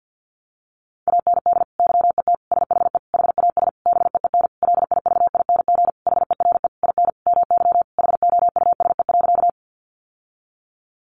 Morse code, heard as a keyed tone, about 50 wpm, 700 Hz. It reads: AND YET SHE HAS BEEN LIVING HERE IN MY HOUSE.
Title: Sentences from Top 1100 Words 50wpm